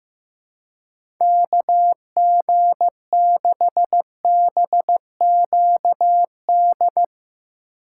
Morse code, heard as a keyed tone, about 15 wpm, 700 Hz. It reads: KG6BQD
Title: USA Call Signs - 6 Characters - 6x Character Spacing - 15wpm